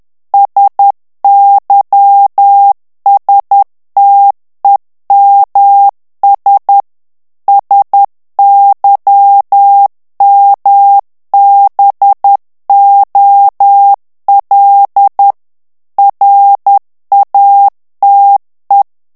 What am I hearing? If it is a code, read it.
Morse code, heard as a keyed tone, about 10 wpm, 800 Hz. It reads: SYSTEMS SYMBOL RATE
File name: ANGLAIS texte clair 3 vitesse 10 mots minute